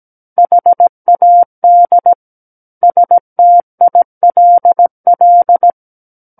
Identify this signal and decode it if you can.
Morse code, heard as a keyed tone, about 17 wpm, 700 Hz. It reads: HAD STILL
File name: Book One / 08-Chapter 8 - 17 WPM